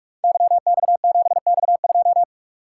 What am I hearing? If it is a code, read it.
Morse code, heard as a keyed tone, about 44 wpm, 700 Hz. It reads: YX7X2
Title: Code Groups - 5 Letter:Numbers - 45wpm